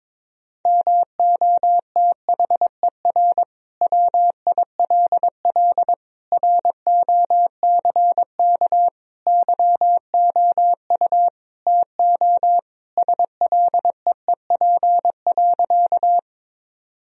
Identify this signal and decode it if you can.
Morse code, heard as a keyed tone, about 22 wpm, 700 Hz. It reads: MOTHER WILL ROCK YOU TO SLEEP.
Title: Sentences from Top 1000 Words - Review Part 2 - 22wpm